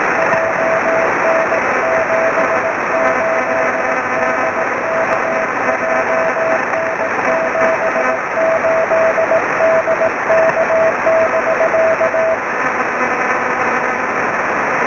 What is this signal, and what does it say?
Morse code, heard as a keyed tone, about 18 wpm, 650 Hz. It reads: JN78DK JN78DK <BK>
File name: sm7lcb / oe5vrl-5 23cm